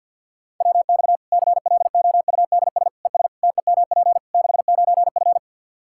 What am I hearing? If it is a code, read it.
Morse code, heard as a keyed tone, about 47 wpm, 700 Hz. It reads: WX CLOUDS ES TEMP 69F
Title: K1USN SST QSO Practice 50wpm